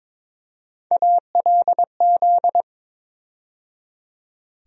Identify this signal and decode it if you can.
Morse code, heard as a keyed tone, about 22 wpm, 700 Hz. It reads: ALZ
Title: USA Call Sign Postfixes - 3 Letters 22wpm